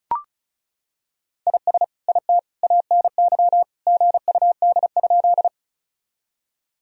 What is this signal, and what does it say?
Morse code, heard as a keyed tone, about 36 wpm, 700 Hz. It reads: IS IT ANY GUD?
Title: Sentences from Top 200 Words and Abbreviations 35wpm